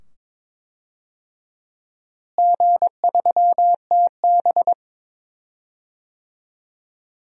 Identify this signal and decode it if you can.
Morse code, heard as a keyed tone, about 22 wpm, 700 Hz. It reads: G3TB
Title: DXCC-100 Call Signs - 4 Characters - Repeated 2x - 22wpm